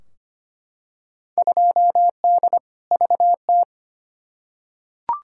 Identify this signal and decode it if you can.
Morse code, heard as a keyed tone, about 25 wpm, 700 Hz. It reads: 2D VT
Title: Field Day Exchanges Encore - 25wpm